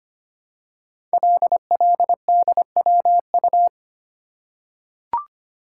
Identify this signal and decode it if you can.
Morse code, heard as a keyed tone, about 25 wpm, 700 Hz. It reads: LLDWU
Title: Code Groups - 5 Letter:Numbers - 25wpm